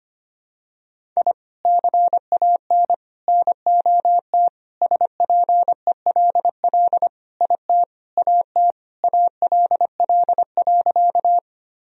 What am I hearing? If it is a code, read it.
Morse code, heard as a keyed tone, about 25 wpm, 700 Hz. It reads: I CAN NOT SPELL IT AT ALL.
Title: Sentences from Top 1000 Words - Review Part 1 - 25wpm